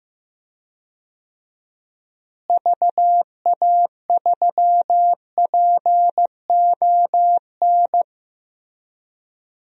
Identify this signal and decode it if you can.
Morse code, heard as a keyed tone, about 15 wpm, 700 Hz. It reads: VA3PON
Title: DXCC-100 Call Signs - 6 Characters - Repeated 3x - 15wpm